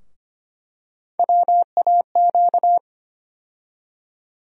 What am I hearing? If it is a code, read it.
Morse code, heard as a keyed tone, about 25 wpm, 700 Hz. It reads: WAQ
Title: USA Call Sign Postfixes - 3 Letters 25wpm